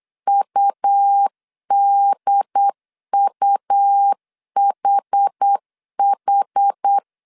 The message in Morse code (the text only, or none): UDUHH